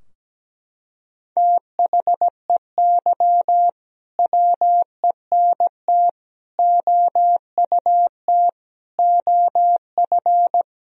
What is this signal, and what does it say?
Morse code, heard as a keyed tone, about 17 wpm, 700 Hz. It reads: THEY WENT OUT OF
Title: Sentences from Top 1000 Words - Review Part 4 - 17wpm